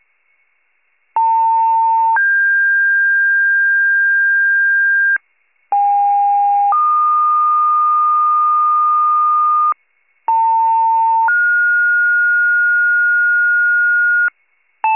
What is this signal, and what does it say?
Two-tone page, heard as a one-second tone, then a three-second tone, about 900 then 1600 Hz, about 810 then 1220 Hz, about 900 then 1430 Hz, with noise. It is heard from 1.2 to 14.3 s.
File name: pager / 73 RESCUE / wcalert